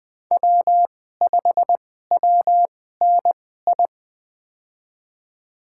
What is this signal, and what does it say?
Morse code, heard as a keyed tone, about 20 wpm, spaced as for 15 wpm, 700 Hz. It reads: W5WNI
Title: USA Call Signs - 5 Characters - 2x Character Spacing - 20wpm